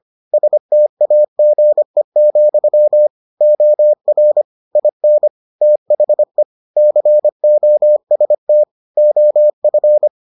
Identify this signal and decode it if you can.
Morse code, heard as a keyed tone, about 25 wpm, 600 Hz. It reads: STAGE, OR IN THE COST OF